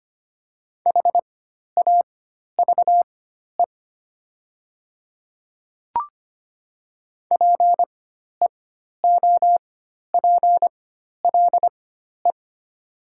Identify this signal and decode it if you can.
Morse code, heard as a keyed tone, about 25 wpm, spaced as for 12 wpm, 700 Hz. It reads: HAVE PEOPLE
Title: 4x Character Spacing - Top 100 Words - 25wpm